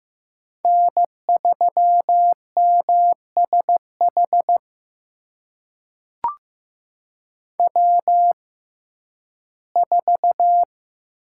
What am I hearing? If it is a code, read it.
Morse code, heard as a keyed tone, about 16 wpm, spaced as for 11 wpm, 700 Hz. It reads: N3MSH W4
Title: USA Call Signs - 5 Characters - 6x Character Spacing - 15wpm